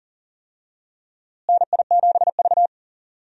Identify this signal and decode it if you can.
Morse code, heard as a keyed tone, about 40 wpm, 700 Hz. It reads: NI7V
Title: USA Call Signs - 4 Characters - Repeated 3x - 40wpm